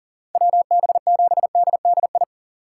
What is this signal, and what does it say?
Morse code, heard as a keyed tone, about 39 wpm, 700 Hz. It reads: WB7DDI
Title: USA Call Signs - 6 Characters - 2x Character Spacing - 40wpm